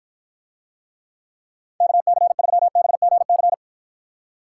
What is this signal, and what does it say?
Morse code, heard as a keyed tone, about 53 wpm, 700 Hz. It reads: KC3BGC